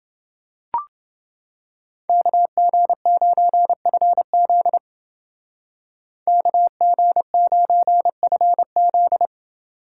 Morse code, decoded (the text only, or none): KG9FZ KG9FZ